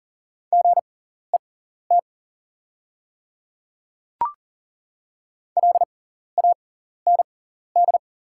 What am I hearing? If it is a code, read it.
Morse code, heard as a keyed tone, about 38 wpm, spaced as for 14 wpm, 690 Hz. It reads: GET LAND
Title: 6x Character Spacing - Top 200 Words - 40wpm